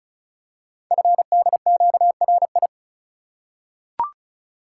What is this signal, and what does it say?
Morse code, heard as a keyed tone, about 36 wpm, 700 Hz. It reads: FDQRI
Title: Code Groups - 5 Letters - 35wpm